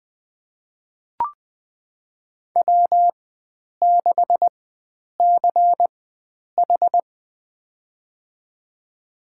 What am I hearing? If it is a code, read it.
Morse code, heard as a keyed tone, about 20 wpm, spaced as for 9 wpm, 700 Hz. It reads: W6CH
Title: USA Call Signs - 4 Characters - 4x Character Spacing - 20wpm